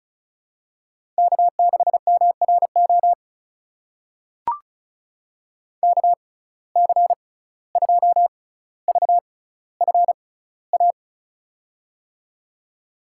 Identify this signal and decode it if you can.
Morse code, heard as a keyed tone, about 36 wpm, spaced as for 17 wpm, 700 Hz. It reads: K6MRO KC2VFA
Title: USA Call Signs - 6x Character Spacing - 35wpm